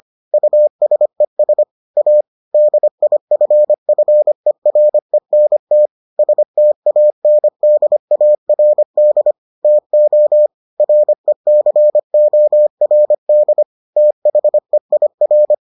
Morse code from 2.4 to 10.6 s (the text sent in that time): DIFFERENT STANDARD TO